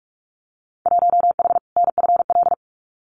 Morse code, heard as a keyed tone, about 44 wpm, 700 Hz. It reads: 1H NFL
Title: Winter Field Day Exchanges 45wpm